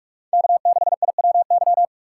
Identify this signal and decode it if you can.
Morse code, heard as a keyed tone, about 46 wpm, 700 Hz. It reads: K6IWY